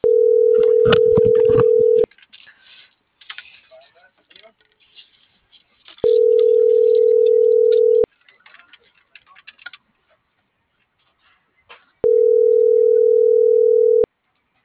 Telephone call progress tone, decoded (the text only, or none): ringback tone